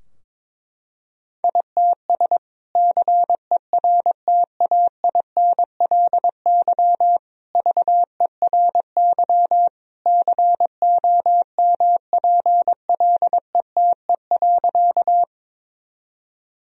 Morse code, heard as a keyed tone, about 21 wpm, 700 Hz. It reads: ITS CERTAINLY VERY COMPLETE.
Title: Sentences from Top 1000 Words - Review Part 2 - 22wpm